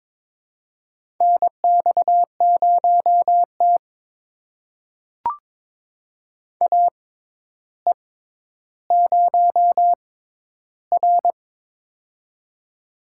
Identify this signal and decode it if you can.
Morse code, heard as a keyed tone, about 22 wpm, spaced as for 11 wpm, 700 Hz. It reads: NX0T AE0R
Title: USA Call Signs - 4 Characters - 6x Character Spacing - 22wpm